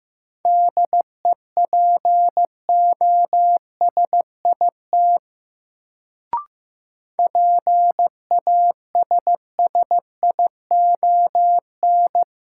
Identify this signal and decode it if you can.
Morse code, heard as a keyed tone, about 15 wpm, 700 Hz. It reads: DEPOSIT PASSION
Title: Sentences from Top 1300 Words - New Word Drill - 15wpm